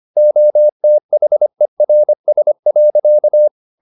Morse code, heard as a keyed tone, about 25 wpm, 600 Hz. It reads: OTHERS.